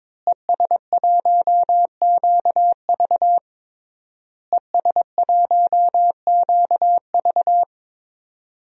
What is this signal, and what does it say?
Morse code, heard as a keyed tone, about 22 wpm, 700 Hz. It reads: ES1QV ES1QV
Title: DXCC-100 Call Signs - 5 Characters - Repeated 3x - 22wpm